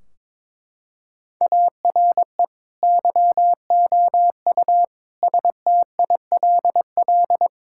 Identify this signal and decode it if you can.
Morse code, heard as a keyed tone, about 21 wpm, 700 Hz. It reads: ARE YOU STILL
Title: Sentences from Top 1600 Words 22wpm